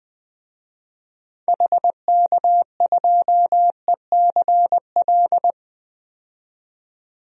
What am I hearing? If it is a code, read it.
Morse code, heard as a keyed tone, about 21 wpm, 700 Hz. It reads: HK2ECL
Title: DXCC-100 Call Signs - 6 Characters - 20wpm